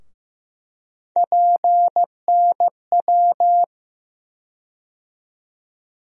Morse code, heard as a keyed tone, about 15 wpm, 700 Hz. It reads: PNW
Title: USA Call Sign Postfixes - 3 Letters 15wpm